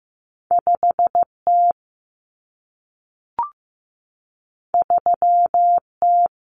Morse code, heard as a keyed tone, about 15 wpm, 700 Hz. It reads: 5T 3T
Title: Common Cut Numbers - 2-Digit 15wpm